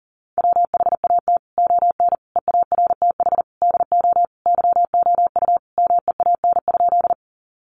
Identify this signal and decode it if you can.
Morse code, heard as a keyed tone, about 40 wpm, 700 Hz. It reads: WHAT ON EARTH DO YOU MEAN?
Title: Sentences from Top 200 Words 40wpm